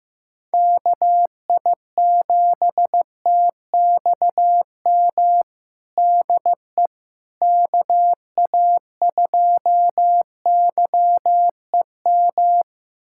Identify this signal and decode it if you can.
Morse code, heard as a keyed tone, about 15 wpm, 700 Hz. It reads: KI7TXM DE KA2YEM
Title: USA Callsign DE Callsign - 6 Characters - 15wpm